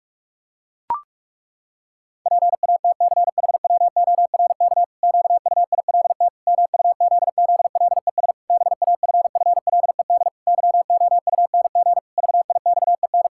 Morse code, heard as a keyed tone, about 41 wpm, 700 Hz. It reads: PATCHWORK QUILT MUZZLES BAFFLED YOUNG VIXEN
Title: Pangram Challenge v2 - 45wpm